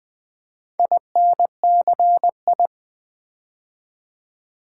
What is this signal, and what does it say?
Morse code, heard as a keyed tone, about 20 wpm, 700 Hz. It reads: INCI
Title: Build A Word - 4 Letters At A Time - 8-Letter Words - 20wpm